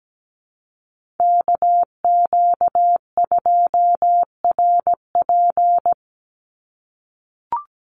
Morse code, heard as a keyed tone, about 17 wpm, 700 Hz. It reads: KQ2RP
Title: NA SOTA Chaser Callsigns 17wpm